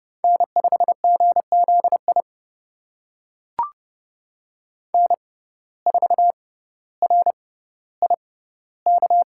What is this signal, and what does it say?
Morse code, heard as a keyed tone, about 30 wpm, spaced as for 15 wpm, 700 Hz. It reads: N5GZI N4RIK
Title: USA Call Signs - 5 Characters - 6x Character Spacing - 30wpm